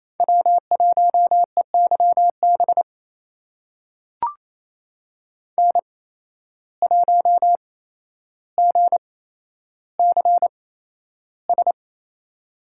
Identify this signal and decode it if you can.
Morse code, heard as a keyed tone, about 28 wpm, spaced as for 12 wpm, 700 Hz. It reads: W1EYB N1GCS
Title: USA Call Signs - 5 Characters - 8x Character Spacing - 28wpm